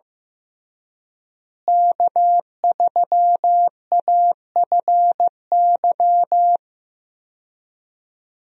Morse code, heard as a keyed tone, about 15 wpm, 700 Hz. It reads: K3AFY
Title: USA Call Signs - 5 Characters - Repeated 2x - 15wpm